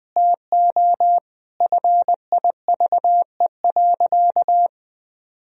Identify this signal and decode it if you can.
Morse code, heard as a keyed tone, about 20 wpm, 700 Hz. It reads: TO FIVE.
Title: Sentences from Top 1000 Words - Review Part 2 - 20wpm